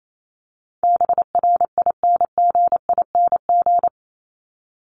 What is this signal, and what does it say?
Morse code, heard as a keyed tone, about 28 wpm, 700 Hz. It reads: BRINGING
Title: ING Suffix - Top 100 Words with Base Words 28wpm